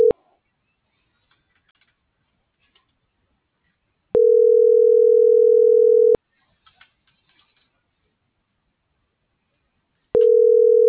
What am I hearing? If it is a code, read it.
Ringback tone (North America), 440 + 480 Hz.